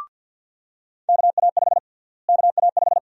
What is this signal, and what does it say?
Morse code, heard as a keyed tone, about 47 wpm, 700 Hz. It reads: KA5 KA5
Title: USA Call Sign Prefixes - Repeated 2x - 50wpm